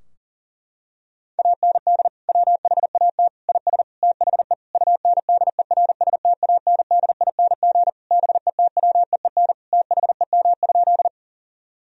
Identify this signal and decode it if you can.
Morse code, heard as a keyed tone, about 40 wpm, 700 Hz. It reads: AND WHAT IS THE UNDERSTANDING BETWEEN THEM?
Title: Sentences from Top 1000 Words - Review Part 3 - 40wpm